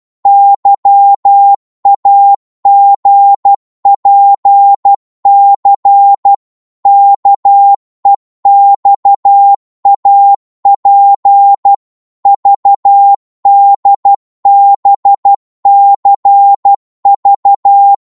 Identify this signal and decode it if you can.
Morse code, heard as a keyed tone, about 12 wpm, 800 Hz. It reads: YAGPC KEXAP VDBCV
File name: Morse / ltr-12wpm